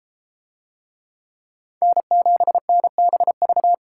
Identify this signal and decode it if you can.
Morse code, heard as a keyed tone, about 30 wpm, 700 Hz. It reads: N7NBV